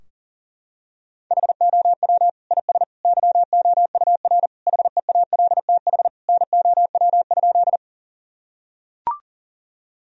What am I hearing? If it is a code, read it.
Morse code, heard as a keyed tone, about 40 wpm, 700 Hz. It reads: HOW IS YOUR HEALTH NOW?
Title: Sentences from Top 1000 Words - Review Part 3 - 40wpm